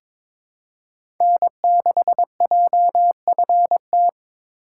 Morse code, heard as a keyed tone, about 22 wpm, 700 Hz. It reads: N6JFT
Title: USA Call Signs - 5 Characters - 3x Character Spacing - 22wpm